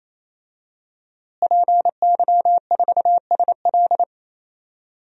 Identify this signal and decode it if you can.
Morse code, heard as a keyed tone, about 28 wpm, 700 Hz. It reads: PY4SL
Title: DXCC-100 Call Signs - 5 Characters - Repeated 3x - 28wpm